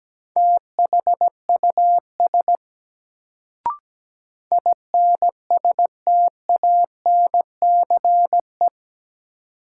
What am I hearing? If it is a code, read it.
Morse code, heard as a keyed tone, about 17 wpm, 700 Hz. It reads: THUS INSTANCE
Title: Sentences from Top 800 Words - New Word Drill - 17wpm